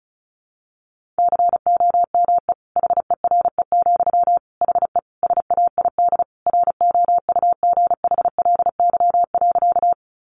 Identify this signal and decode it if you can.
Morse code, heard as a keyed tone, about 34 wpm, 700 Hz. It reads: COME HERE, HE SAID ROUGHLY.
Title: Sentences from Top 1500 Words 35wpm